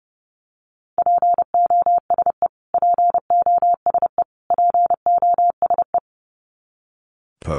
Morse code, heard as a keyed tone, about 30 wpm, 700 Hz. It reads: POSE POSE POSE
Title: Top 500 Words Repeated 30wpm